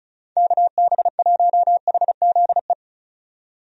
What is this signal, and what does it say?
Morse code, heard as a keyed tone, about 34 wpm, 700 Hz. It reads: KB1HZE
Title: USA Call Signs - 6 Characters - 35wpm